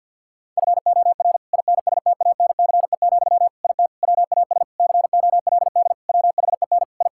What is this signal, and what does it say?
Morse code, heard as a keyed tone, about 50 wpm, 700 Hz. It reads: FOR INSTANCE, IT WAS COLD WHEN I
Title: Sentences from Top 1000 Words - Review Part 5 - 50wpm